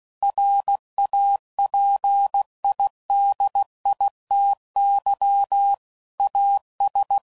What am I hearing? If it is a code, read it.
Morse code, heard as a keyed tone, about 16 wpm, 800 Hz. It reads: RAPIDITY AS
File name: CW Code Groups @15WPM / Technical Subjects / XC